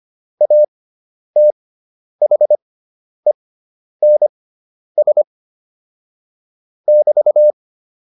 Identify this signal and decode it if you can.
Morse code, heard as a keyed tone, about 25 wpm, spaced as for 10 wpm, 600 Hz. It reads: ATHENS <BT>
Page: intermediate-practice-files